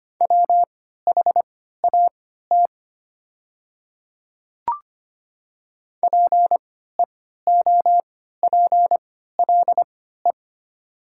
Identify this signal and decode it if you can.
Morse code, heard as a keyed tone, about 25 wpm, spaced as for 14 wpm, 700 Hz. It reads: WHAT PEOPLE
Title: 3x Character Spacing - Top 100 Words - 25wpm